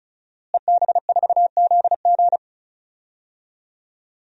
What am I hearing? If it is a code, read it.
Morse code, heard as a keyed tone, about 36 wpm, 700 Hz. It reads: EB4ZG
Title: DXCC-100 Call Signs - 5 Characters - Repeated 3x - 35wpm